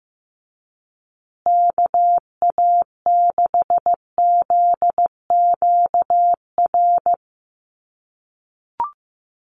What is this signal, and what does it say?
Morse code, heard as a keyed tone, about 15 wpm, 700 Hz. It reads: KA6ZQR
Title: USA Call Signs Repeated 3x 15wpm